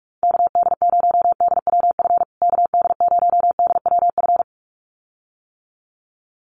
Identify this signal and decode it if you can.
Morse code, heard as a keyed tone, about 44 wpm, 700 Hz. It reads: KD0DWF KD0DWF
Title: USA Call Signs Repeated 2x 45wpm